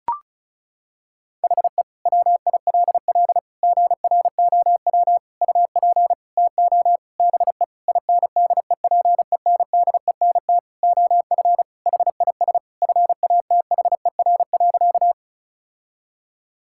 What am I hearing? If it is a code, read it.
Morse code, heard as a keyed tone, about 36 wpm, 700 Hz. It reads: HE WILL GROW UP TO BE INDEPENDENT OF HIS FATHER.
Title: Sentences from Top 1000 Words - Review Part 6 - 35wpm